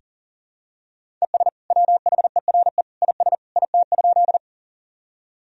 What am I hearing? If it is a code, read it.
Morse code, heard as a keyed tone, about 40 wpm, 700 Hz. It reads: ES WHERE IS IT?
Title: Sentences from Top 200 Words and Abbreviations 40wpm